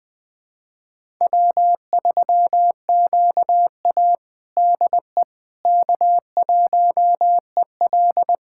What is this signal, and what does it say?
Morse code, heard as a keyed tone, about 20 wpm, 700 Hz. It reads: W3QA DE K1EL
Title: USA Callsign DE Callsign - 4 Characters - 20wpm